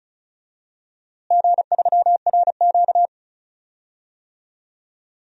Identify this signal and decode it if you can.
Morse code, heard as a keyed tone, about 36 wpm, 700 Hz. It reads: G3RQ
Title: DXCC-100 Call Signs - 4 Characters - Repeated 3x - 35wpm